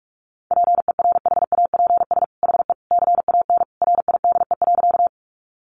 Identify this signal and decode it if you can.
Morse code, heard as a keyed tone, about 44 wpm, 700 Hz. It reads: PERHAPS HE CAN RIDE.
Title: Sentences from Top 1200 Words 45wpm